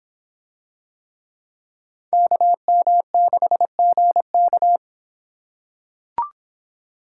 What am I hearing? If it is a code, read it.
Morse code, heard as a keyed tone, about 23 wpm, 700 Hz. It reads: KM6GK